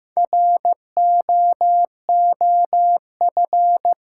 Morse code, heard as a keyed tone, about 15 wpm, 700 Hz. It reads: ROOF
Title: Four Letter Words - 15wpm